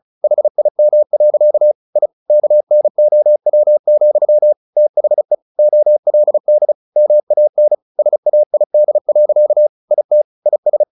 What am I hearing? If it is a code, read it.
Morse code, heard as a keyed tone, about 34 wpm, 600 Hz. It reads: HIM. I KNOW, THE OLD MAN SAID. IT IS
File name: the old man and the sea / the old man and the sea 1 35wpm